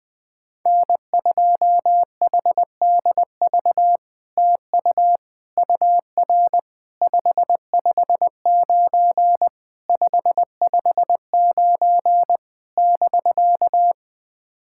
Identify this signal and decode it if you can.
Morse code, heard as a keyed tone, about 20 wpm, 700 Hz. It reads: N2HDV TU UR 559 559 <BK>
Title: SOTA QSO Practice 20wpm